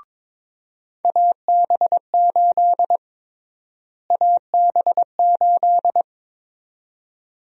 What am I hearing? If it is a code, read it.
Morse code, heard as a keyed tone, about 23 wpm, 700 Hz. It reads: AB8 AB8
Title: USA Call Sign Prefixes - Repeated 2x - 22wpm